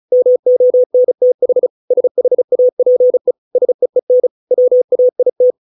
Morse code, heard as a keyed tone, about 34 wpm, 500 Hz. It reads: MONTH SHAPE SEEN WAIT